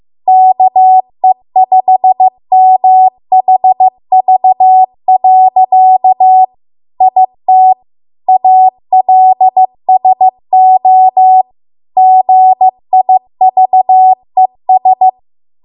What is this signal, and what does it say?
Morse code, heard as a keyed tone, about 15 wpm, 750 Hz. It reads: KE5MHV. IT ALSO GIVES